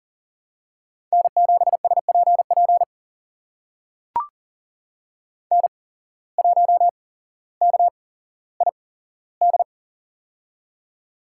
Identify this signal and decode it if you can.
Morse code, heard as a keyed tone, about 41 wpm, spaced as for 17 wpm, 700 Hz. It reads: N7SPP N1KID
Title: USA Call Signs - 5 Characters - 8x Character Spacing - 40wpm